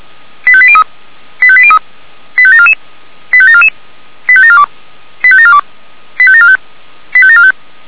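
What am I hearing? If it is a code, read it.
Five-tone selective call (ZVEI), heard as quick runs of five tones, tones of about 70 ms. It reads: 85803 85803 85744 85744 85742 85742 85745 85745
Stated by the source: ZVEI five-tone selective call received on an SDR.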